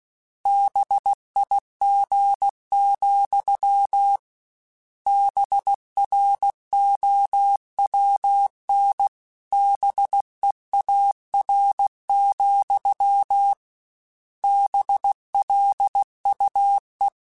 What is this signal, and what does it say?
Morse code, heard as a keyed tone, about 16 wpm, 800 Hz. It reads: BIG, BROWN BEAR, BLUE